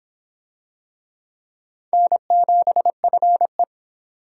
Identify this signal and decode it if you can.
Morse code, heard as a keyed tone, about 26 wpm, 700 Hz. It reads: N7FE